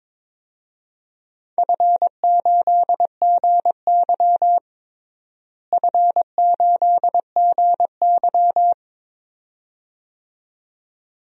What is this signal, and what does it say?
Morse code, heard as a keyed tone, about 22 wpm, 700 Hz. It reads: F8GY F8GY
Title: DXCC-100 Call Signs - 4 Characters - Repeated 2x - 22wpm